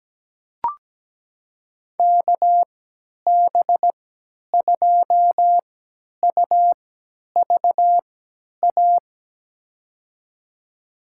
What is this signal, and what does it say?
Morse code, heard as a keyed tone, about 17 wpm, spaced as for 10 wpm, 700 Hz. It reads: KB2UVA
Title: USA Call Signs - 6 Characters - 3x Character Spacing - 17wpm